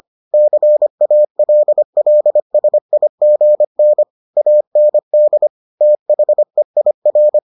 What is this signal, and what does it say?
Morse code, heard as a keyed tone, about 25 wpm, 600 Hz. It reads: CALLSIGN AND THEIR